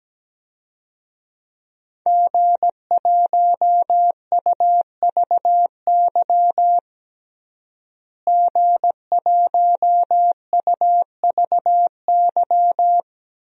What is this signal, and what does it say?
Morse code, heard as a keyed tone, about 17 wpm, 700 Hz. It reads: G1UVY G1UVY
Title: DXCC-100 Call Signs - 5 Characters - Repeated 2x - 17wpm